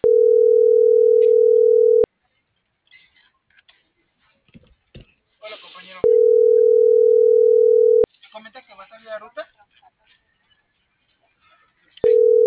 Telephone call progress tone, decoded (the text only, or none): ringback tone